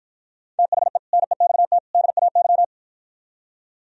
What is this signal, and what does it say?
Morse code, heard as a keyed tone, about 53 wpm, 700 Hz. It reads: THE NEXT DAY